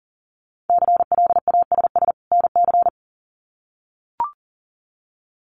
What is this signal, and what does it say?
Morse code, heard as a keyed tone, about 40 wpm, 700 Hz. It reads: CLASS NC
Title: Winter Field Day Exchanges 40wpm